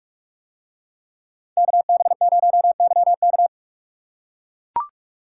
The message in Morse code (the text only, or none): KB0YK